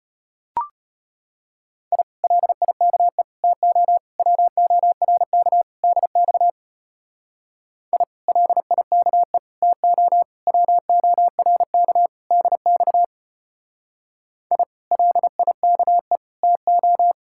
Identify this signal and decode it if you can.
Morse code, heard as a keyed tone, about 31 wpm, 700 Hz. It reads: I LIKE TO WORK DX I LIKE TO WORK DX I LIKE TO
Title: Speed-Racing Common QSO Phrases - 25wpm